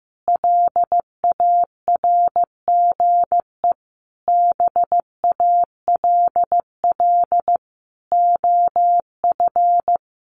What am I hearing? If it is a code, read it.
Morse code, heard as a keyed tone, about 15 wpm, 700 Hz. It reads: LARGE BALL OF
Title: Easy Phrases 15wpm